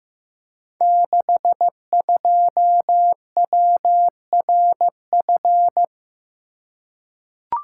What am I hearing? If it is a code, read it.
Morse code, heard as a keyed tone, about 15 wpm, 700 Hz. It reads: 62WRF
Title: Code Groups - 5 Letter:Numbers - 15wpm